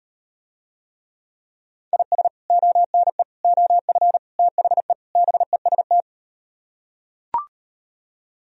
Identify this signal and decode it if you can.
Morse code, heard as a keyed tone, about 38 wpm, 700 Hz. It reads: IS ONE OF THE BEST